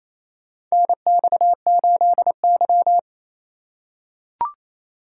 Morse code, heard as a keyed tone, about 28 wpm, 700 Hz. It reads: NX8Y
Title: USA Call Signs - 4 Characters - 4x Character Spacing - 28wpm